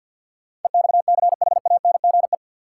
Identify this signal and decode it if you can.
Morse code, heard as a keyed tone, about 47 wpm, 700 Hz. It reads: EXCHANGE
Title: Sets of 5 Words - New Word Drill - 50wpm